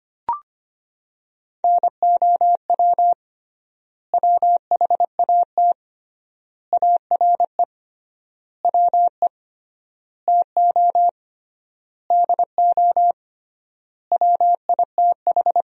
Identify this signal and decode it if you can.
Morse code, heard as a keyed tone, about 25 wpm, 700 Hz. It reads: NOW WHAT ARE WE TO DO WITH
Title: 3x Word-Spacing - Sentences from Top 100 Words - 25wpm